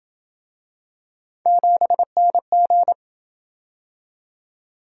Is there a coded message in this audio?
Yes: Morse code from 1.5 to 2.9 s, heard as a keyed tone, about 27 wpm, 700 Hz. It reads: 7NG